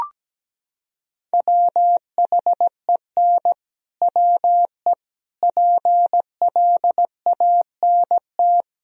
Morse code, heard as a keyed tone, about 17 wpm, 700 Hz. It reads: WHEN WE PLANT
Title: Sentences from Top 1000 Words - Review Part 1 - 17wpm